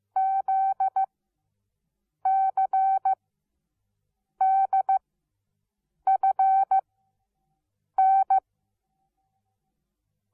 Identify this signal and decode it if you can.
Morse code, heard as a keyed tone, about 15 wpm, spaced as for 6 wpm, 770 Hz. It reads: ZCDFN
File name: Onlinelehrgang / morsen / DJ4UF Morsen17